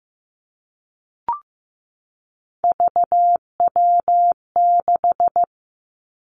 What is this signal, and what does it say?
Morse code, heard as a keyed tone, about 15 wpm, 700 Hz. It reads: VW6
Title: Three Challenging Letter-Number Combinations 15wpm